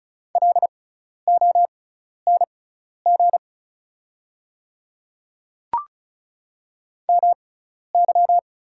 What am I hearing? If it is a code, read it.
Morse code, heard as a keyed tone, about 35 wpm, spaced as for 12 wpm, 690 Hz. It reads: LONG MY
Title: 6x Character Spacing - Top 100 Words - 35wpm